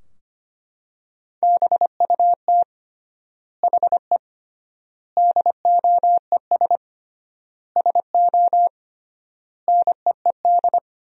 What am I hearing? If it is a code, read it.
Morse code, heard as a keyed tone, about 25 wpm, 700 Hz. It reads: BUT HE DOES SO NEED
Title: 3x Word-Spacing - Sentences from Top 200 Words - 25wpm